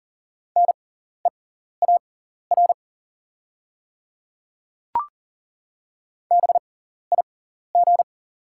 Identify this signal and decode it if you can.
Morse code, heard as a keyed tone, about 40 wpm, spaced as for 14 wpm, 700 Hz. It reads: NEAR BIG
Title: 6x Character Spacing - Top 300 Words - 40wpm